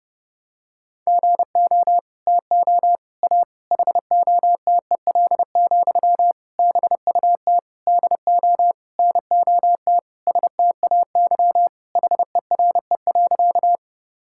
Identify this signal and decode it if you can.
Morse code, heard as a keyed tone, about 30 wpm, 700 Hz. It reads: GO TO A HOTEL, BUT DO NOT STAY HERE.
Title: Sentences from Top 1600 Words 30wpm